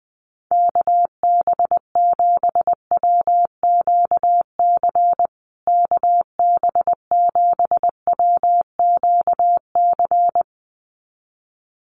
Morse code, heard as a keyed tone, about 20 wpm, 700 Hz. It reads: KB7WQC KB7WQC
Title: USA Call Signs Repeated 2x 20wpm